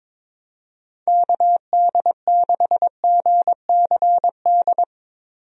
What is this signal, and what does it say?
Morse code, heard as a keyed tone, about 22 wpm, 700 Hz. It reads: KD6GCD
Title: USA Call Signs - 6 Characters - Repeated 2x - 22wpm